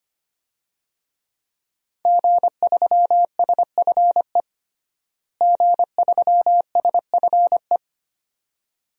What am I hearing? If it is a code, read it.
Morse code, heard as a keyed tone, about 25 wpm, 700 Hz. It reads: G3SFE G3SFE
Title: DXCC-100 Call Signs - 5 Characters - Repeated 2x - 25wpm